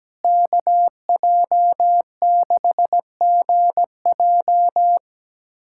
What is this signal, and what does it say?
Morse code, heard as a keyed tone, about 17 wpm, 700 Hz. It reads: KJ6GJ
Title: USA Callsign DE Callsign 17wpm